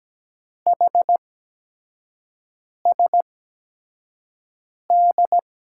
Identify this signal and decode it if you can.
Morse code, heard as a keyed tone, about 17 wpm, spaced as for 5 wpm, 700 Hz. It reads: HSD
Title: USA Call Signs - 6 Characters - 8x Character Spacing - 17wpm